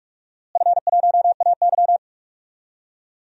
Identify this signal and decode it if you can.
Morse code, heard as a keyed tone, about 46 wpm, 700 Hz. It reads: F1AY
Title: DXCC-100 Call Signs - 4 Characters - Repeated 3x - 45wpm